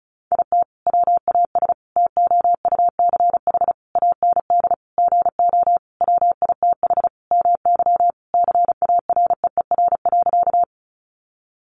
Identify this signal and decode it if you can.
Morse code, heard as a keyed tone, about 34 wpm, 700 Hz. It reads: IT WAS TOUCH AND GO WITH MY CAREER.
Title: Sentences from Top 1200 Words 35wpm